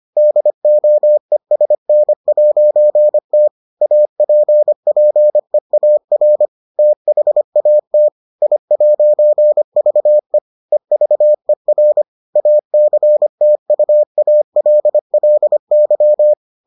Morse code, heard as a keyed tone, about 25 wpm, 600 Hz. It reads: DOESN'T APPEAR THAT I'VE EVER ACTUALLY